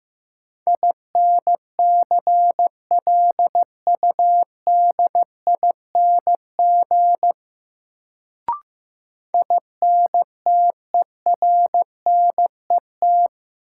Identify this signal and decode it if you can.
Morse code, heard as a keyed tone, about 15 wpm, 700 Hz. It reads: INCLUDING INTERNET
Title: Sentences from Top 1000 Words - Review Part 4 - New Word Drill - 15wpm